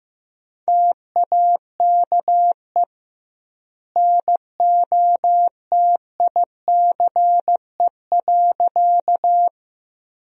Morse code, heard as a keyed tone, about 15 wpm, 700 Hz. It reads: TAKE NOTICE.
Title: 2x Word-Spacing - Sentences from Top 400 Words - 15wpm